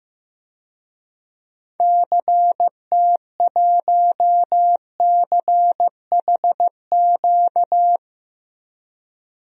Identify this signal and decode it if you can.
Morse code, heard as a keyed tone, about 15 wpm, 700 Hz. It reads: CT1CHQ
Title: DXCC-100 Call Signs - 6 Characters - 15wpm